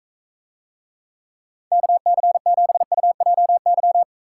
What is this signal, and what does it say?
Morse code, heard as a keyed tone, about 44 wpm, 700 Hz. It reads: KC7UJY